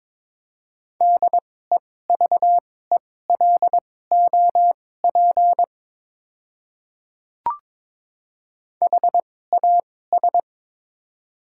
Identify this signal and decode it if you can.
Morse code, heard as a keyed tone, about 22 wpm, spaced as for 16 wpm, 700 Hz. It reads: DEVELOP HAS
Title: 2x Character Spacing - Top 500 Words - 22wpm